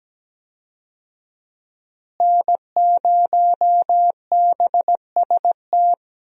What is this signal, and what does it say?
Morse code, heard as a keyed tone, about 17 wpm, 700 Hz. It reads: N0BST
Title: USA Call Signs - 5 Characters - Repeated 2x - 17wpm